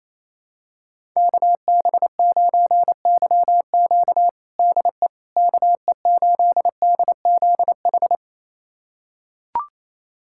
Morse code, heard as a keyed tone, about 28 wpm, 700 Hz. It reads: KB9YQ DE KE8DZH
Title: USA Callsign DE Callsign 28wpm